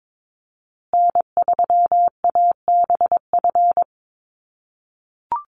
Morse code, heard as a keyed tone, about 22 wpm, 700 Hz. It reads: N3ABF
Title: USA Call Signs Repeated 2x 22wpm